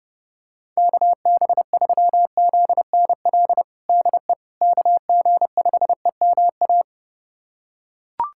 Morse code, heard as a keyed tone, about 32 wpm, 700 Hz. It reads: KB3ZNL DE KG5EMA
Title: USA Callsign DE Callsign - 6 Characters - 30wpm